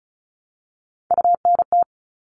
Morse code, heard as a keyed tone, about 34 wpm, 700 Hz. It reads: UNT
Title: All Cut Numbers - 3-Digit 35wpm